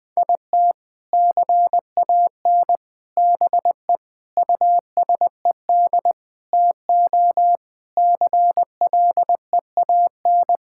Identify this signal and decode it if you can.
Morse code, heard as a keyed tone, about 21 wpm, 700 Hz. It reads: IT CAN BE USED TO CLEAN
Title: Sentences with 4-Word Phrases - 20wpm